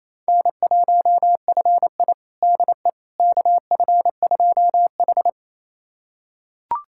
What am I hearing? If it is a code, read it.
Morse code, heard as a keyed tone, about 28 wpm, 700 Hz. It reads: N1FI DE KF2H
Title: USA Callsign DE Callsign - 4 Characters - 28wpm